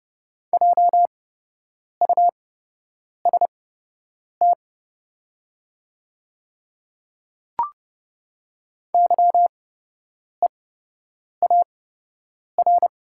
Morse code, heard as a keyed tone, about 31 wpm, spaced as for 8 wpm, 700 Hz. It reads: JUST YEAR
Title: 8x Character Spacing - Top 200 Words - 30wpm